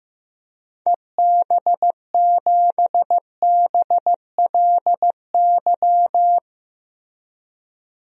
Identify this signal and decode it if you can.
Morse code, heard as a keyed tone, about 15 wpm, 700 Hz. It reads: EB7BLY
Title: DXCC-100 Call Signs - 6 Characters - 15wpm